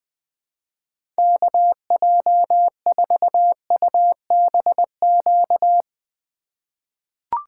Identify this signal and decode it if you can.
Morse code, heard as a keyed tone, about 20 wpm, 700 Hz. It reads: KJ4UBQ
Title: USA Call Signs - 6 Characters - 3x Character Spacing - 20wpm